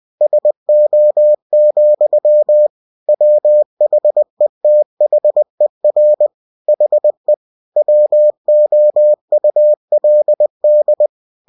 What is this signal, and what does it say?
Morse code, heard as a keyed tone, about 20 wpm, 600 Hz. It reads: SO, WHETHER HE WOULD
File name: cw-books / winnie-the-pooh / winnie-20wpm-page-0022